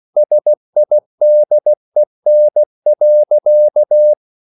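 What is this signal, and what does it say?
Morse code, heard as a keyed tone, about 16 wpm, 600 Hz. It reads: SIDEN.
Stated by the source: Slow (QRS) Morse code at 16 wpm, from CW med Gnister.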